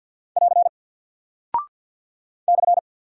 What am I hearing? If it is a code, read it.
Morse code, heard as a keyed tone, about 48 wpm, 700 Hz. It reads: <AR> /
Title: Single Character (Letter:Number:Punctuation:Prosign) Alt 50wpm